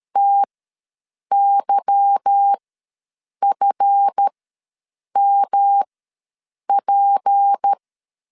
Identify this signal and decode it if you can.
Morse code, heard as a keyed tone, about 13 wpm, spaced as for 7 wpm, 800 Hz. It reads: TYFMP
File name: LEZIONI / 8 YJ45 / 8 YJ4 ws 12wpm 3249